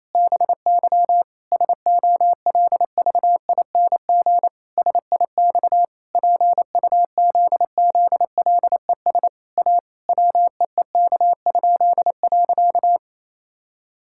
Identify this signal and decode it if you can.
Morse code, heard as a keyed tone, about 28 wpm, 700 Hz. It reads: BY SOLVING SIX PUZZLES A WEEK?.
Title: Pangram Challenge v2 - 28wpm